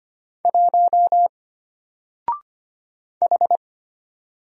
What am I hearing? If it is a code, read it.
Morse code, heard as a keyed tone, about 25 wpm, 700 Hz. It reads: 1 H
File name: Single-Letter-Number-25wpm